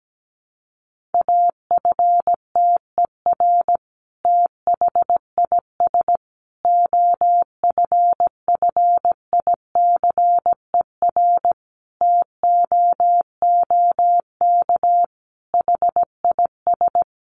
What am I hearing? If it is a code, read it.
Morse code, heard as a keyed tone, about 17 wpm, 700 Hz. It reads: AFTER THIS OFFICER TOOK HIS
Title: Sentences from Top 1200 Words 17wpm